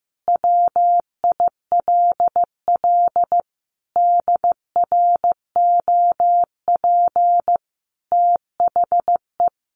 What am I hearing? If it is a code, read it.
Morse code, heard as a keyed tone, about 15 wpm, 700 Hz. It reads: WILL DROP THE
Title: Sentences from Top 1400 Words 15wpm